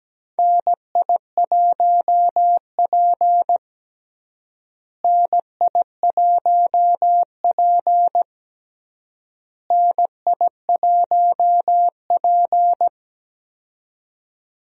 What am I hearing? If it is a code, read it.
Morse code, heard as a keyed tone, about 17 wpm, 700 Hz. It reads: NI1P NI1P NI1P
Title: USA Call Signs - 4 Characters - Repeated 3x - 17wpm